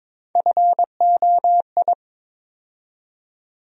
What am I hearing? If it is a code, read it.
Morse code, heard as a keyed tone, about 22 wpm, 700 Hz. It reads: FOI
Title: USA Call Sign Postfixes - 3 Letters 22wpm